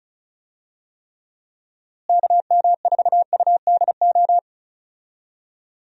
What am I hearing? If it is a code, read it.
Morse code, heard as a keyed tone, about 34 wpm, 700 Hz. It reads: KM4UDO
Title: USA Call Signs - 6 Characters - Repeated 2x - 35wpm